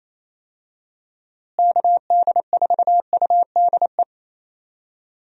Morse code, heard as a keyed tone, about 28 wpm, 700 Hz. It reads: KD4UDE
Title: USA Call Signs - 6 Characters - Repeated 3x - 28wpm